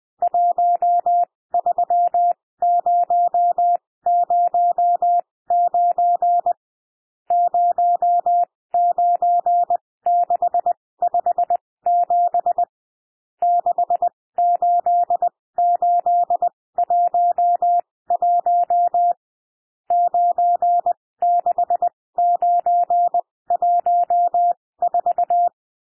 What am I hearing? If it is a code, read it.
Morse code, heard as a keyed tone, about 20 wpm, spaced as for 16 wpm, 700 Hz. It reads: 13009 09657 68811 96914
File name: cwmp3 / N1 SOUND NUMBERS 20 wpm